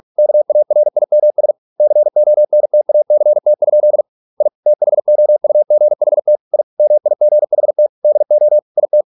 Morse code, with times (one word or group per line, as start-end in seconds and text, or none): 0.2-1.5 KARIMS
1.8-4.0 CONTACT?
4.4-4.5 I
4.7-6.4 THOUGHT
6.5-6.6 I
6.8-7.9 MIGHT
8.0-8.6 DO
8.8-9.0 IT